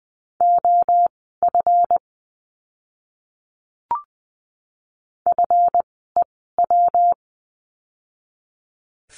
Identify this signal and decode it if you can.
Morse code, heard as a keyed tone, about 20 wpm, spaced as for 15 wpm, 700 Hz. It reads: OF FEW
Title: 206 Lesson 33f - Letter j - Words 20wpm